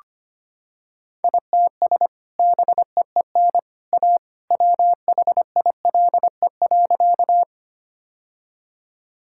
Morse code, heard as a keyed tone, about 25 wpm, 700 Hz. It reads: ITS BEEN A WHILE.
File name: practice / 25wpm / Sentences with 2-Word Phrases - 25wpm